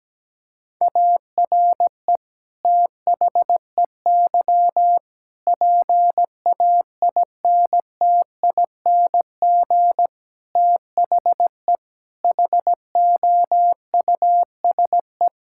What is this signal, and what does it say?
Morse code, heard as a keyed tone, about 17 wpm, 700 Hz. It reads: ARE THEY PAINTING THE HOUSE
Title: Sentences from Top 1000 Words - Review Part 6 - 17wpm